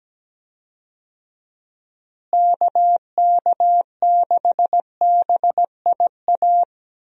Morse code, heard as a keyed tone, about 17 wpm, 700 Hz. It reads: KK6BIA
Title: USA Call Signs - 6 Characters - Repeated 3x - 17wpm